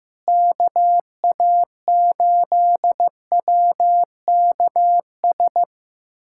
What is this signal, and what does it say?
Morse code, heard as a keyed tone, about 15 wpm, 700 Hz. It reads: KA8WKS
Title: USA Callsign DE Callsign 15wpm